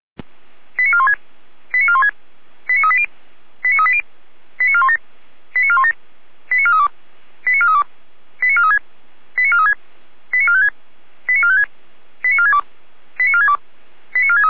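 Five-tone selective call (ZVEI), heard as quick runs of five tones, tones of about 70 ms. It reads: 89416 89416 89380 89380 89417 89417 89432 89432 89436 89436 89456 89456 89462 89462 89463
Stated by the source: ZVEI five-tone call 89356 received on SDR.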